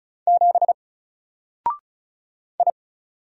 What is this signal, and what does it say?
Morse code, heard as a keyed tone, about 34 wpm, 700 Hz. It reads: 7 I
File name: Single-Letter-Number-35wpm